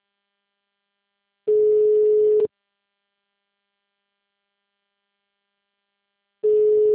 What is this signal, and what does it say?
Ringback tone (Europe, 425 Hz).